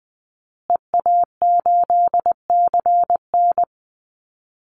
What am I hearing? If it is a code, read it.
Morse code, heard as a keyed tone, about 20 wpm, 700 Hz. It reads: EA8CN
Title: Call Signs Repeated 2x - 20wpm